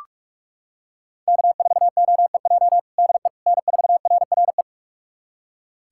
Morse code, heard as a keyed tone, about 44 wpm, 700 Hz. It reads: K4OEJ DE N4RRE
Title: USA Callsign DE Callsign - 5 Characters - 45wpm